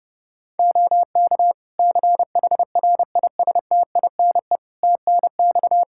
Morse code, heard as a keyed tone, about 30 wpm, 700 Hz. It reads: OK CHRISTINE TNX
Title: Practice QSO - Exchange 4 - Ending - 30wpm